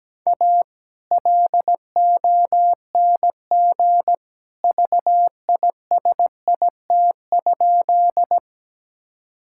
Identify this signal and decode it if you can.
Morse code, heard as a keyed tone, about 17 wpm, 700 Hz. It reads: A LONG VISIT?
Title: Sentences from Top 1000 Words - Review Part 6 - 17wpm